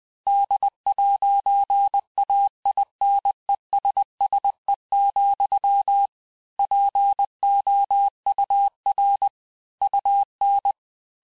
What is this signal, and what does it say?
Morse code, heard as a keyed tone, about 20 wpm, 800 Hz. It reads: D'AINESSE, POUR UN
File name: cours-du-mercredi / audios / cours-858-texte-clais-vitesse-19mpm